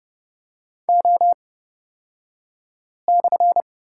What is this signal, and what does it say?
Morse code, heard as a keyed tone, about 30 wpm, spaced as for 5 wpm, 700 Hz. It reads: O/
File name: Lesson 37d - Letter x - Two Characters 30wpm